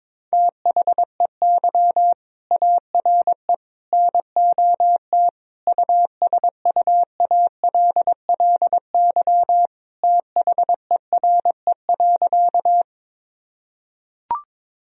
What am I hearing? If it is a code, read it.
Morse code, heard as a keyed tone, about 22 wpm, 700 Hz. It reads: THEY ARE NOT USUALLY THERE.
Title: Sentences from Top 1000 Words - Review Part 3 - 22wpm